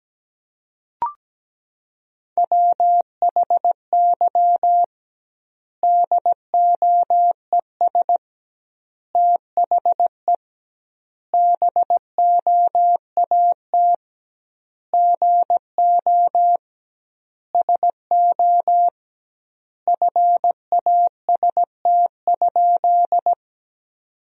Morse code, heard as a keyed tone, about 17 wpm, 700 Hz. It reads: WHY DOES THE BOAT GO SO FAST?
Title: 2x Word-Spacing - Sentences from Top 500 Words - 17wpm